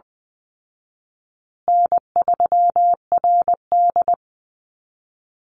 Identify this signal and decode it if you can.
Morse code, heard as a keyed tone, about 20 wpm, 700 Hz. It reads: N3RD
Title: Easy Call Signs Repeated 2x - 20wpm